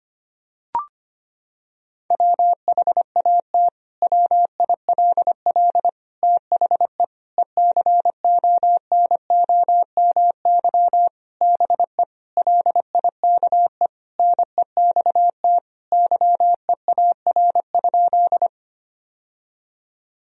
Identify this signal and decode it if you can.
Morse code, heard as a keyed tone, about 25 wpm, 700 Hz. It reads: WHAT WILL THE ECONOMY BE LIKE NEXT YEAR?
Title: Sentences from Top 1000 Words - Review Part 3 - 25wpm